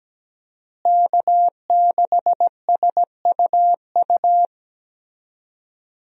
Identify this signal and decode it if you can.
Morse code, heard as a keyed tone, about 17 wpm, 700 Hz. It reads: K6SUU
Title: USA Call Signs - 5 Characters - Repeated 3x - 17wpm